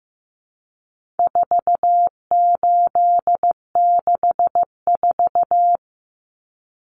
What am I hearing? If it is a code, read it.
Morse code, heard as a keyed tone, about 15 wpm, 700 Hz. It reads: 4864
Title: Numbers - 4-Digit 15wpm